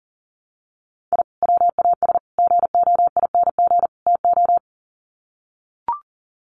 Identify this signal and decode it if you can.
Morse code, heard as a keyed tone, about 40 wpm, 700 Hz. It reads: I WAS GOING TO
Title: Sets of 4 Words 40wpm